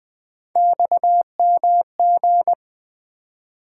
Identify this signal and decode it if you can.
Morse code, heard as a keyed tone, about 20 wpm, 700 Hz. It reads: XMG
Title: Three Letter-Number Combinations - 20wpm